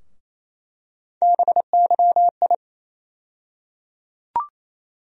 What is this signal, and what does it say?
Morse code, heard as a keyed tone, about 28 wpm, 700 Hz. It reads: BYI
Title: USA Call Sign Postfixes - 3 Letters 28wpm